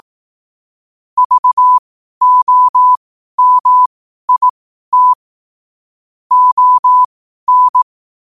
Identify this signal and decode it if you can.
Morse code, heard as a keyed tone, about 18 wpm, spaced as for 12 wpm, 1000 Hz. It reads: VOMIT ON
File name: qotdfiles / 12wpm